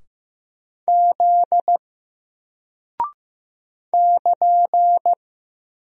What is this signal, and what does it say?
Morse code, heard as a keyed tone, about 15 wpm, 700 Hz. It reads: Z <KN>
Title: Single Character (Letter:Number:Punctuation:Prosign) Alt 15wpm